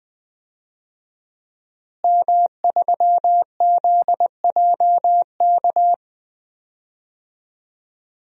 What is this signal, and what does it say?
Morse code, heard as a keyed tone, about 20 wpm, 700 Hz. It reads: M3ZJK
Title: DXCC-100 Call Signs - 5 Characters - Repeated 2x - 20wpm